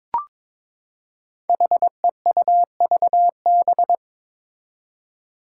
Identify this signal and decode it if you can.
Morse code, heard as a keyed tone, about 22 wpm, 700 Hz. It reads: HEUVB
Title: Five Challenging Letter-Number Combinations 22wpm